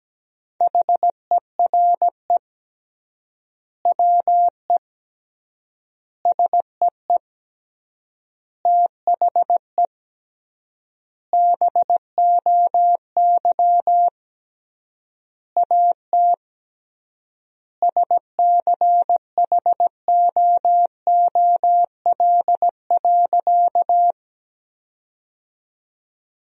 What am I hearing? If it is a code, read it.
Morse code, heard as a keyed tone, about 17 wpm, 700 Hz. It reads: HERE WE SEE THE BOY AT SCHOOL.
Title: 3x Word-Spacing - Sentences from Top 300 Words - 17wpm